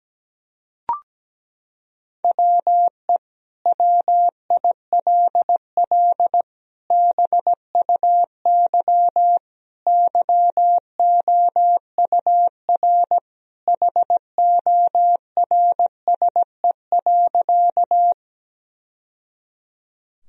Morse code, heard as a keyed tone, about 17 wpm, 700 Hz. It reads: WE WILL BUY YOUR HORSE.
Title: Sentences from Top 1000 Words - Review Part 4 - 17wpm